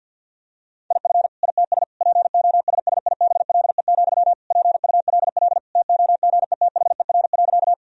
Morse code, heard as a keyed tone, about 50 wpm, 700 Hz. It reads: IF ITS POSSIBLE, PULL TOGETHER.
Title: Sentences from Top 1000 Words - Review Part 2 - 50wpm